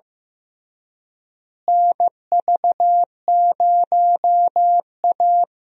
Morse code, heard as a keyed tone, about 15 wpm, 700 Hz. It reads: NV0A
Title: USA Call Signs - 4 Characters - Repeated 2x - 15wpm